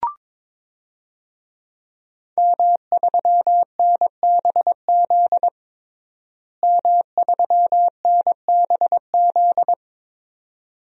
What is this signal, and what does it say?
Morse code, heard as a keyed tone, about 22 wpm, 700 Hz. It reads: M3NBZ M3NBZ
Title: DXCC-100 Call Signs - 5 Characters - Repeated 2x - 22wpm